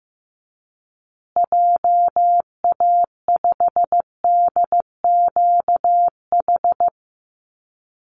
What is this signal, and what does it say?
Morse code, heard as a keyed tone, about 15 wpm, 700 Hz. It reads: JA5DQH
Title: Call Signs Repeated 3x - 15wpm